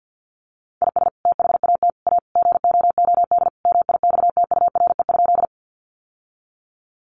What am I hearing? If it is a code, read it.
Morse code, heard as a keyed tone, about 50 wpm, 700 Hz. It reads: IS THAT A GOOD MIXTURE?
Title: Sentences from Top 1500 Words 50wpm